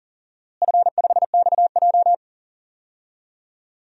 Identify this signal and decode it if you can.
Morse code, heard as a keyed tone, about 40 wpm, 700 Hz. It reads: F5XJ
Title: DXCC-100 Call Signs - 4 Characters - Repeated 2x - 40wpm